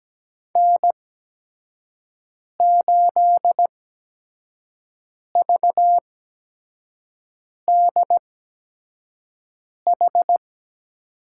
Morse code, heard as a keyed tone, about 17 wpm, spaced as for 5 wpm, 700 Hz. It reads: N8VDH
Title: USA Call Signs - 5 Characters - 8x Character Spacing - 17wpm